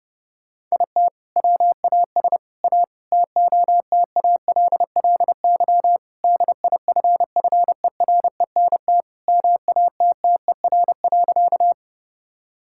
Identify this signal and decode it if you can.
Morse code, heard as a keyed tone, about 30 wpm, 700 Hz. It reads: IT WAS A TOTALLY DIFFERENT MATTER.
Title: Sentences from Top 1600 Words 30wpm